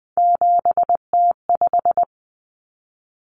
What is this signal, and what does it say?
Morse code, heard as a keyed tone, about 20 wpm, 700 Hz. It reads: 7T5
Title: Common Cut Numbers - 3-Digit 20wpm